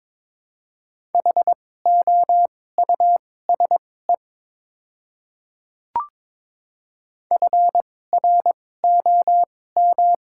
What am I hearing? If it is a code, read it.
Morse code, heard as a keyed tone, about 22 wpm, spaced as for 16 wpm, 700 Hz. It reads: HOUSE FROM
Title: 2x Character Spacing - Top 200 Words - 22wpm